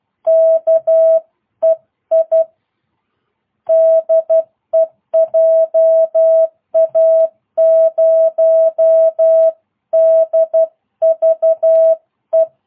Morse code, heard as a keyed tone, about 12 wpm, 650 Hz. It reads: KEI DE JA0DVE